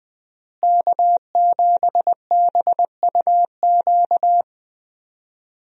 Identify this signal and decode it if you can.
Morse code, heard as a keyed tone, about 20 wpm, 700 Hz. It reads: K7BUQ
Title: USA Call Signs - 5 Characters - Repeated 2x - 20wpm